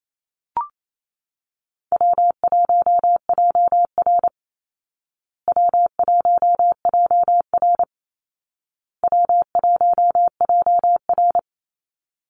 Morse code, heard as a keyed tone, about 28 wpm, 700 Hz. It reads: W1JR W1JR W1JR
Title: Easy Call Signs Repeated 3x - 28wpm